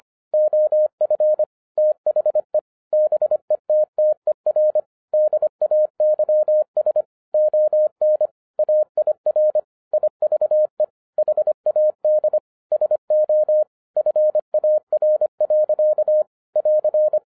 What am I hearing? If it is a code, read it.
Morse code, heard as a keyed tone, about 25 wpm, 610 Hz. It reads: OF THE BETTER DAYS ON AIR IVE HAD SO FAR. <AR>
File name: FIRST FISTS CONTACT-25wpm